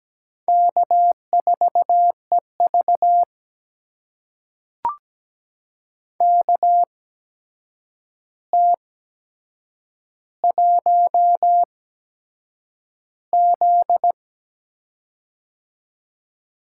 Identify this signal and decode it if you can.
Morse code, heard as a keyed tone, about 17 wpm, spaced as for 7 wpm, 700 Hz. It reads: K4EV KT1Z
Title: USA Call Signs - 4 Characters - 8x Character Spacing - 17wpm